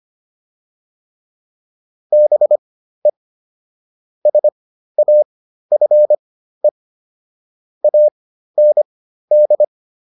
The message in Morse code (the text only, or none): BE SAFE AND